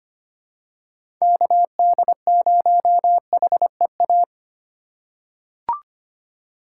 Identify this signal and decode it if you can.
Morse code, heard as a keyed tone, about 25 wpm, 700 Hz. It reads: KD0HEA
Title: USA Call Signs - 6 Characters - 2x Character Spacing - 25wpm